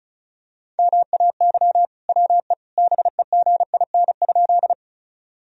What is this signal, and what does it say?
Morse code, heard as a keyed tone, about 34 wpm, 700 Hz. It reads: MAY WE BEGIN?
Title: Sentences from Top 1000 Words - Review Part 1 - 35wpm